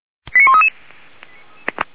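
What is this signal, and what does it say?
Five-tone selective call (ZVEI), heard as quick runs of five tones, tones of about 70 ms. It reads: 89133
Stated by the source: Five-tone ZVEI selective call received on an SDR.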